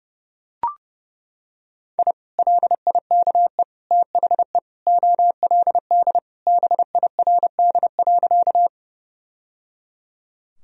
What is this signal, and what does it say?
Morse code, heard as a keyed tone, about 30 wpm, 700 Hz. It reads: I LIKE THE OLD BIRD.
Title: Sentences from Top 1000 Words - Review Part 2 - 30wpm